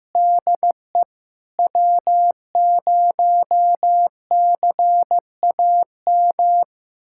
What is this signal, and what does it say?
Morse code, heard as a keyed tone, about 16 wpm, 700 Hz. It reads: DE W0CAM
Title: USA Callsign DE Callsign - 5 Characters - 15wpm